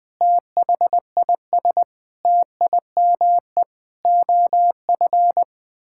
Morse code, heard as a keyed tone, about 21 wpm, 700 Hz. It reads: THIS TIME OF
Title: Sentences from Top 1000 Words - Review Part 6 - 20wpm